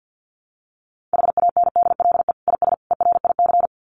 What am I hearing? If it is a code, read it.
Morse code, heard as a keyed tone, about 50 wpm, 700 Hz. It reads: HANDLE IS ERIC
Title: Common QSO Phrases 50wpm